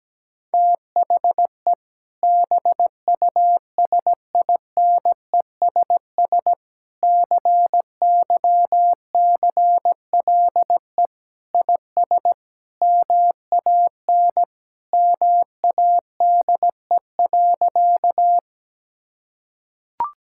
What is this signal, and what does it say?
Morse code, heard as a keyed tone, about 17 wpm, 700 Hz. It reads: THE BUSINESS CYCLE IS MAN MADE.
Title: Sentences from Top 1000 Words - Review Part 6 - 17wpm